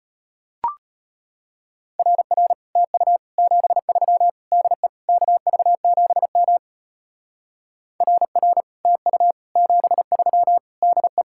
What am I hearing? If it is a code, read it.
Morse code, heard as a keyed tone, about 31 wpm, 700 Hz. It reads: RR TU 73 DE KV7M RR TU 73 DE
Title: Speed-Racing Common QSO Phrases - 25wpm